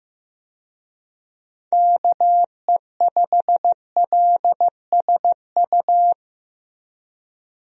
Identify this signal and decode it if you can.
Morse code, heard as a keyed tone, about 15 wpm, 700 Hz. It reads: KE5LSU
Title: USA Call Signs - 6 Characters - Repeated 2x - 15wpm